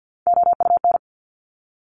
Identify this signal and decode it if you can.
Morse code, heard as a keyed tone, about 50 wpm, 700 Hz. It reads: OUN
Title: Top Three Letter Combinations in Words 50wpm